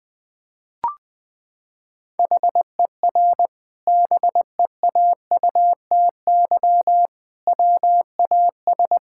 Morse code, heard as a keyed tone, about 20 wpm, 700 Hz. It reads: HER BEAUTY WAS
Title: Sentences from Top 1000 Words - Review Part 6 - 20wpm